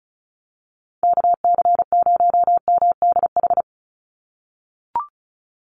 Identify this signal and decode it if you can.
Morse code, heard as a keyed tone, about 34 wpm, 700 Hz. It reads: KC0MDH
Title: USA Call Signs 35wpm